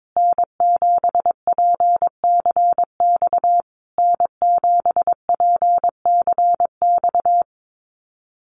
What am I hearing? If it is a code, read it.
Morse code, heard as a keyed tone, about 22 wpm, 700 Hz. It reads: N7PCX N7PCX
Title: USA Call Signs Repeated 3x 22wpm